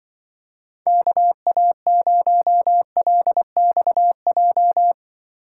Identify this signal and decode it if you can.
Morse code, heard as a keyed tone, about 25 wpm, 700 Hz. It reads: KA0LXJ
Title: Speed Racing - USA 2x3 Call Signs 20wpm